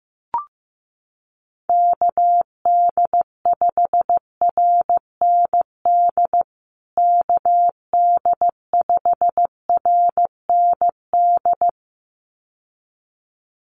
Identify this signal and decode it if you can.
Morse code, heard as a keyed tone, about 15 wpm, 700 Hz. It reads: KD5RND KD5RND
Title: USA Call Signs Repeated 2x 15wpm